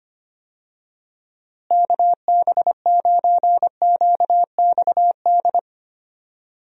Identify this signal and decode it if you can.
Morse code, heard as a keyed tone, about 25 wpm, 700 Hz. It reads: KB9QXD
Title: USA Call Signs - 6 Characters - Repeated 3x - 25wpm